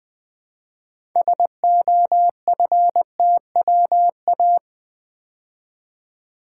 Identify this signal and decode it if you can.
Morse code, heard as a keyed tone, about 20 wpm, 700 Hz. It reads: SOFTWA
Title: Build A Word - 2 Letters At A Time - 8-Letter Words - 20wpm